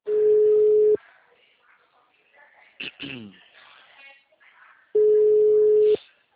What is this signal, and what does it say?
Ringback tone (Europe, 425 Hz), with speech.